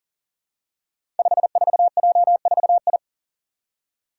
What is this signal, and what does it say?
Morse code, heard as a keyed tone, about 40 wpm, 700 Hz. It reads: 54J4I
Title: Five Challenging Letter-Number Combinations 40wpm